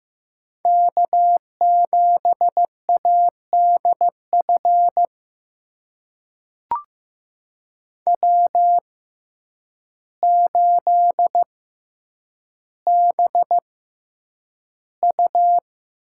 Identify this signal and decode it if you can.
Morse code, heard as a keyed tone, about 15 wpm, spaced as for 8 wpm, 700 Hz. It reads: K7ADF W8BU
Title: USA Call Signs - 6x Character Spacing - 15wpm